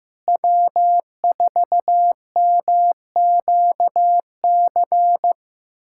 Morse code, heard as a keyed tone, about 15 wpm, 700 Hz. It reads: W4MQC
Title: USA Callsign DE Callsign 15wpm